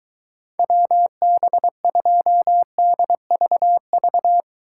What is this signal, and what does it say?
Morse code, heard as a keyed tone, about 23 wpm, 700 Hz. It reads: WB2DVV